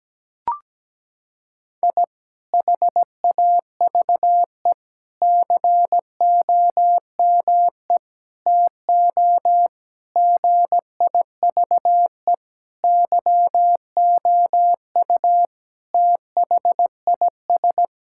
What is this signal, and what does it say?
Morse code, heard as a keyed tone, about 17 wpm, 700 Hz. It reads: I HAVE COME TO GIVE YOU THIS
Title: Sentences from Top 1000 Words - Review Part 6 - 17wpm